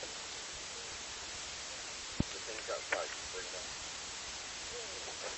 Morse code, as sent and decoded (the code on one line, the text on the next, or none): none